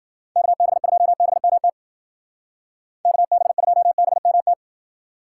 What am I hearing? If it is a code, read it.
Morse code, heard as a keyed tone, about 58 wpm, 700 Hz. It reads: KB2BMT KB2BMT